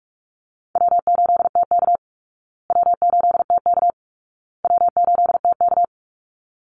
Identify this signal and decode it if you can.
Morse code, heard as a keyed tone, about 44 wpm, 700 Hz. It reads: W8TK W8TK W8TK
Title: Easy Call Signs Repeated 3x - 45wpm